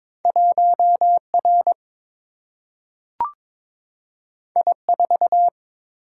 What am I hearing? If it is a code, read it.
Morse code, heard as a keyed tone, about 22 wpm, 700 Hz. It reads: 1R I4
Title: Single-Letter Single-Number 22wpm